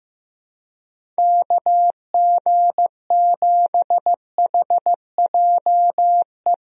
Morse code, heard as a keyed tone, about 15 wpm, 700 Hz. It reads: KG7HJE
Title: USA Call Signs - 6 Characters - Repeated 3x - 15wpm